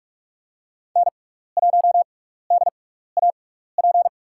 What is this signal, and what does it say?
Morse code, heard as a keyed tone, about 44 wpm, spaced as for 15 wpm, 700 Hz. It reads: N1DAP
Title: USA Call Signs - 5 Characters - 6x Character Spacing - 45wpm